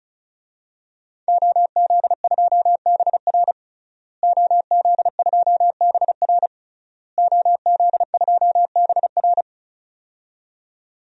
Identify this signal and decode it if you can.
Morse code, heard as a keyed tone, about 36 wpm, 700 Hz. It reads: OZ2BR OZ2BR OZ2BR
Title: DXCC-100 Call Signs - 5 Characters - Repeated 3x - 35wpm